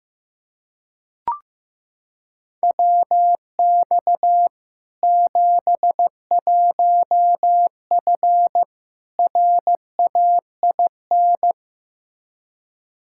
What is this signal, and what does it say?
Morse code, heard as a keyed tone, about 15 wpm, 700 Hz. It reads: WX 71F RAIN
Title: K1USN SST QSO Practice 15wpm